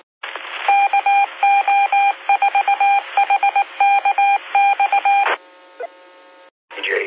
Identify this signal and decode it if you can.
Morse code, heard as a keyed tone, about 18 wpm, 800 Hz. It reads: KO4HKX
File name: SimplexLogic